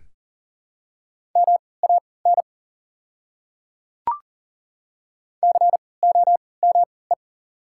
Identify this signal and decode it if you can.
Morse code, heard as a keyed tone, about 40 wpm, spaced as for 23 wpm, 700 Hz. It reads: MAN COME
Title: 3x Character Spacing - Top 300 Words - 40wpm